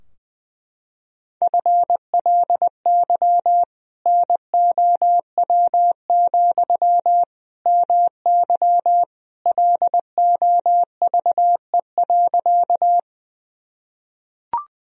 Morse code, heard as a keyed tone, about 20 wpm, 700 Hz. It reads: FLY NOW, MY LOVE.
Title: Sentences from Top 1000 Words - Review Part 2 - 20wpm